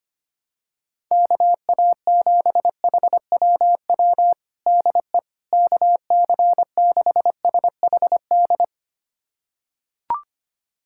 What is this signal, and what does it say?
Morse code, heard as a keyed tone, about 25 wpm, 700 Hz. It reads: KA7HWW DE KC6SHD
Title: USA Callsign DE Callsign 25wpm